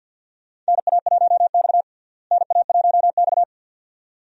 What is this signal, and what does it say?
Morse code, heard as a keyed tone, about 49 wpm, 700 Hz. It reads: NA1X NA1X
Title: USA Call Signs - 4 Characters - Repeated 2x - 50wpm